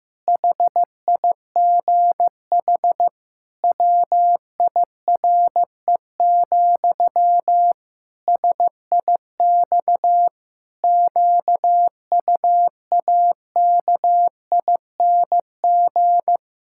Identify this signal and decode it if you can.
Morse code, heard as a keyed tone, about 15 wpm, 700 Hz. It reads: HIGH WIRE, SIX QUAKING
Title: Pangram Challenge v2 - 15wpm